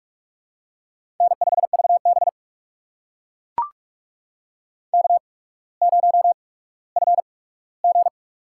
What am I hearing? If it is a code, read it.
Morse code, heard as a keyed tone, about 44 wpm, spaced as for 19 wpm, 700 Hz. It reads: N5VB K0FG
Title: USA Call Signs - 4 Characters - 8x Character Spacing - 45wpm